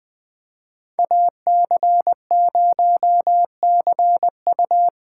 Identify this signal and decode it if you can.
Morse code, heard as a keyed tone, about 20 wpm, 700 Hz. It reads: AC0CU
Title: 2023 POTA Hunter Call Signs - 20wpm